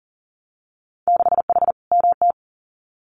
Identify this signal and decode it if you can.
Morse code, heard as a keyed tone, about 40 wpm, 700 Hz. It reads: 6H MT
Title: Winter Field Day Exchanges Encore 40wpm